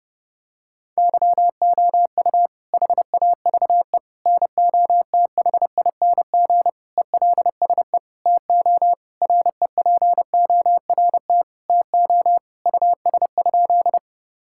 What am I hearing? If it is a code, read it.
Morse code, heard as a keyed tone, about 30 wpm, 700 Hz. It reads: YOU HAVE NOTHING ELSE TO REPORT TO US?
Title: Sentences from Top 1000 Words - Review Part 6 - 30wpm